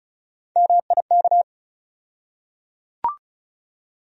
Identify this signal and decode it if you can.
Morse code, heard as a keyed tone, about 36 wpm, 700 Hz. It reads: MIK
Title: USA Call Sign Postfixes - 3 Letters 35wpm